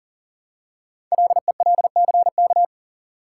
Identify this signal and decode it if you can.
Morse code, heard as a keyed tone, about 43 wpm, 700 Hz. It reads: LELCK
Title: Code Groups - 5 Letters - 40wpm